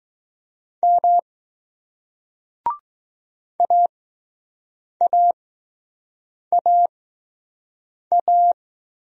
Morse code, heard as a keyed tone, about 20 wpm, spaced as for 6 wpm, 700 Hz. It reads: M AAAA